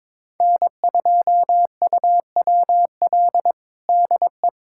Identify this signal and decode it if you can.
Morse code, heard as a keyed tone, about 22 wpm, 700 Hz. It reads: N2UWL DE
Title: USA Callsign DE Callsign - 5 Characters - 22wpm